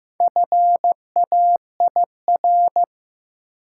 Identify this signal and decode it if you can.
Morse code, heard as a keyed tone, about 15 wpm, 700 Hz. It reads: FAIR
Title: Sentences from Top 1000 Words - Review Part 5 - New Word Drill - 15wpm